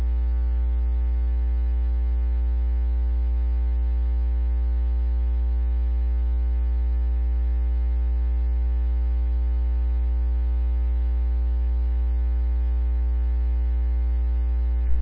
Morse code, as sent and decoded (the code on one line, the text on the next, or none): none